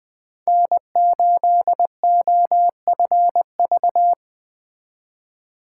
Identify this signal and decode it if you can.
Morse code, heard as a keyed tone, about 20 wpm, 700 Hz. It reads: N8OFV
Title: Speed Racing - USA 1x3 Call Signs 17wpm